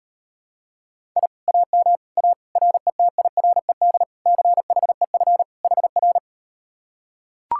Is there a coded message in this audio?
Yes: Morse code from 1.2 to 6.2 s, heard as a keyed tone, about 38 wpm, 700 Hz. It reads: I AM A RETIRED CHEF HR